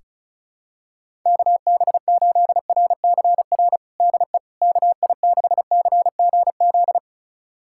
Morse code, heard as a keyed tone, about 36 wpm, 700 Hz. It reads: KB8RCR DE KI6CGZ
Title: USA Callsign DE Callsign - 6 Characters - 35wpm